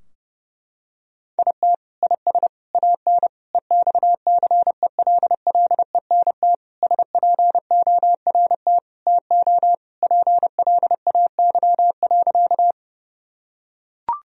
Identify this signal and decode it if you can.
Morse code, heard as a keyed tone, about 30 wpm, 700 Hz. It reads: IT IS AN EXCELLENT SPORT TO PLAY.
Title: Sentences from Top 1000 Words - Review Part 4 - 30wpm